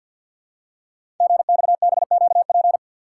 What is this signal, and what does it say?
Morse code, heard as a keyed tone, about 48 wpm, 700 Hz. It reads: GXBQP
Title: Code Groups - 5 Letters - 50wpm